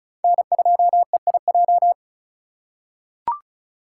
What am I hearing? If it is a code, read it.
Morse code, heard as a keyed tone, about 36 wpm, 700 Hz. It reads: N2EIJ
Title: USA Call Signs - 3x Character Spacing - 35wpm